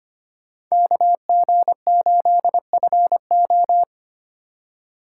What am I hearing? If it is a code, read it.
Morse code, heard as a keyed tone, about 25 wpm, 700 Hz. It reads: KG8FO
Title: USA Call Signs - 5 Characters - 4x Character Spacing - 25wpm